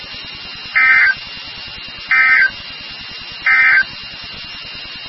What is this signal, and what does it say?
Emergency Alert System (SAME) tones, heard as harsh buzzing bursts. It is heard from 0.6 to 3.9 s.
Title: NOAA Weather Radio WXK36